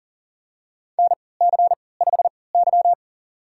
Morse code, heard as a keyed tone, about 37 wpm, spaced as for 22 wpm, 700 Hz. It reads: NC5Y
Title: USA Call Signs - 4 Characters - 3x Character Spacing - 40wpm